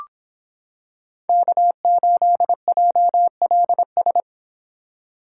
Morse code, heard as a keyed tone, about 25 wpm, 700 Hz. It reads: K8JLS